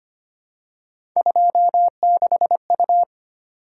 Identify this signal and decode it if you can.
Morse code, heard as a keyed tone, about 25 wpm, 700 Hz. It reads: 26U
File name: Three Letter-Number Combinations - 25wpm